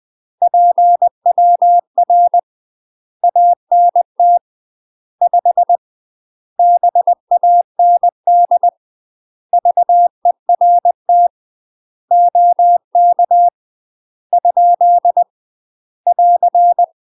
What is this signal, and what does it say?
Morse code, heard as a keyed tone, about 20 wpm, 700 Hz. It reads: PWR ANT 5 BAND VERT OK ? <AR>
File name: UK MASTER QSOs TRAINING / 20WPM / UK MASTER QSOs 0053